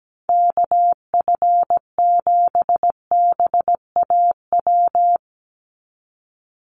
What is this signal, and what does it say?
Morse code, heard as a keyed tone, about 17 wpm, 700 Hz. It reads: KF7BAW
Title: USA Call Signs 17wpm